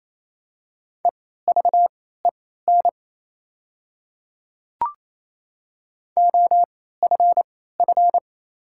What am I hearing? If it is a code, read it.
Morse code, heard as a keyed tone, about 27 wpm, spaced as for 16 wpm, 700 Hz. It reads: EVEN OFF
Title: 3x Character Spacing - Top 200 Words - 28wpm